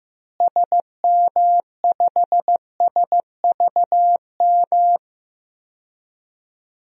Morse code, heard as a keyed tone, about 15 wpm, 700 Hz. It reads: SM5SVM
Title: DXCC-100 Call Signs - 6 Characters - Repeated 2x - 15wpm